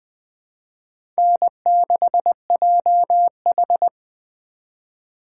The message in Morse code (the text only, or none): N6JH